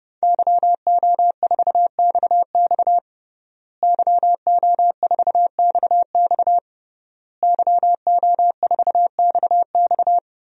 Morse code, heard as a keyed tone, about 30 wpm, 700 Hz. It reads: YO4XX YO4XX YO4XX
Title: DXCC-100 Call Signs - 5 Characters - Repeated 3x - 30wpm